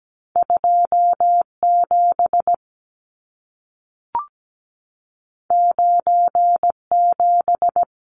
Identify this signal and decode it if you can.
Morse code, heard as a keyed tone, about 17 wpm, 700 Hz. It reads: 27 97
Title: Numbers - 2-Digit 17wpm